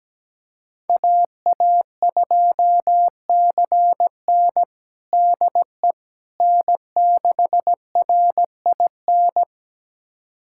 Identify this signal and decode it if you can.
Morse code, heard as a keyed tone, about 17 wpm, 700 Hz. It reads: AA2CN DE N6RIN
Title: USA Callsign DE Callsign - 5 Characters - 17wpm